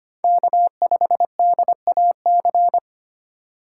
Morse code, heard as a keyed tone, about 25 wpm, 700 Hz. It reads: K5DAC
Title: USA Call Signs - 4x Character Spacing - 25wpm